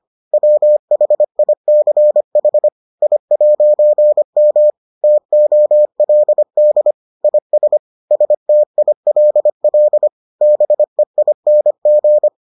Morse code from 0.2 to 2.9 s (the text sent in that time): WHICH